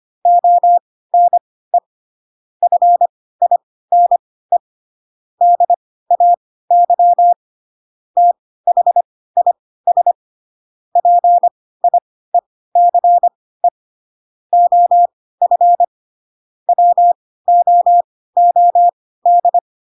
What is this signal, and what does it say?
Morse code, heard as a keyed tone, about 25 wpm, spaced as for 16 wpm, 700 Hz. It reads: ONE FINE DAY THIS PIECE OF WOOD
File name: Pinocchio / Pinocchio ch1 10 25 ik1hge